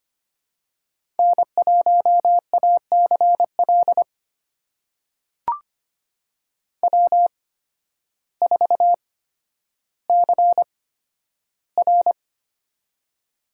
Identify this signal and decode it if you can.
Morse code, heard as a keyed tone, about 25 wpm, spaced as for 12 wpm, 700 Hz. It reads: N1ACL W4CR
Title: USA Call Signs - 8x Character Spacing - 25wpm